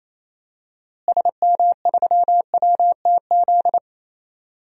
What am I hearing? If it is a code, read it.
Morse code, heard as a keyed tone, about 28 wpm, 700 Hz. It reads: SM3WTZ
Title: DXCC-100 Call Signs - 6 Characters - Repeated 2x - 28wpm